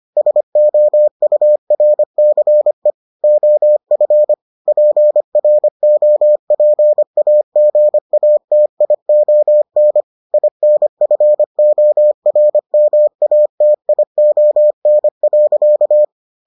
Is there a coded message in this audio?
Yes: Morse code from 0.2 to 16.0 s, heard as a keyed tone, about 25 wpm, 600 Hz. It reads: SOURCE OF PROPAGATION INFORMATION.